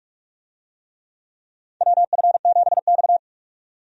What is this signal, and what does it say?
Morse code, heard as a keyed tone, about 46 wpm, 700 Hz. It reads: WF7X
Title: USA Call Signs - 4 Characters - 45wpm